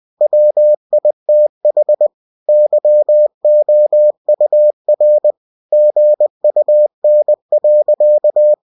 Morse code, heard as a keyed tone, about 20 wpm, 600 Hz. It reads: WITH YOUR GUN.